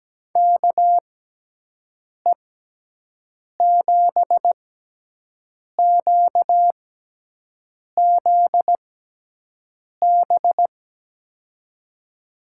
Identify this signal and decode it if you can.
Morse code, heard as a keyed tone, about 17 wpm, 700 Hz. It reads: K E 7 Q Z B
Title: USA Call Signs - 6 Characters - 6x Character Spacing - 17wpm